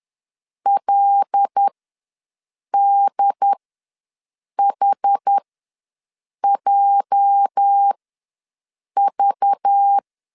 Morse code, heard as a keyed tone, about 10 wpm, spaced as for 6 wpm, 800 Hz. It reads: LDHJV
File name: LEZIONI / 8 YJ45 / 8 YJ4 ws 10wpm 4020